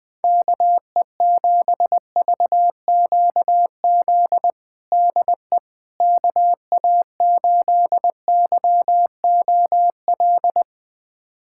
Morse code, heard as a keyed tone, about 21 wpm, 700 Hz. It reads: KE7VQZ DE KA8YOL
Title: USA Callsign DE Callsign - 6 Characters - 20wpm